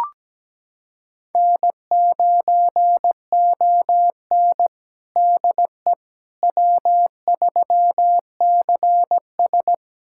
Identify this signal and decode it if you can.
Morse code, heard as a keyed tone, about 17 wpm, 700 Hz. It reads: N9ON DE W3CS
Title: USA Callsign DE Callsign - 4 Characters - 17wpm